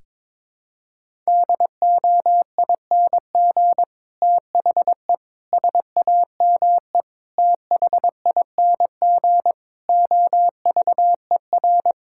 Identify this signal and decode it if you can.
Morse code, heard as a keyed tone, about 22 wpm, 700 Hz. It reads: DOING THE SAME THING OVER
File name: practice / 22wpm / Sets of 8 Words - 22wpm